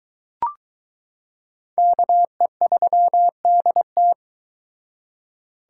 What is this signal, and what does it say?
Morse code, heard as a keyed tone, about 22 wpm, 700 Hz. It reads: KE3DT